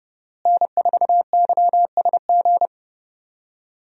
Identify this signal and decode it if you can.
Morse code, heard as a keyed tone, about 31 wpm, 700 Hz. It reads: N4YSG
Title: USA Call Signs - 5 Characters - Repeated 3x - 30wpm